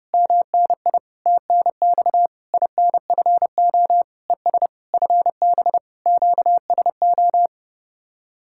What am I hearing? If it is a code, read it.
Morse code, heard as a keyed tone, about 33 wpm, 700 Hz. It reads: MNI TNX INFO ES FB QSO